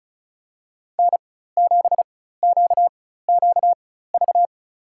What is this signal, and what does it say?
Morse code, heard as a keyed tone, about 36 wpm, spaced as for 16 wpm, 700 Hz. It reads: N7QQV
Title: USA Call Signs - 5 Characters - 4x Character Spacing - 35wpm